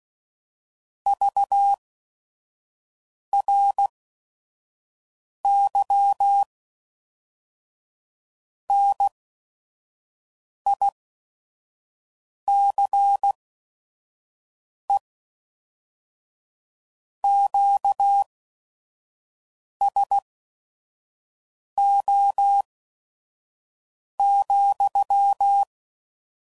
Morse code, heard as a keyed tone, about 16 wpm, spaced as for 5 wpm, 800 Hz. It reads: VRY NICE QSO,
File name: CW Code Groups @5WPM / QSOs / kc7sac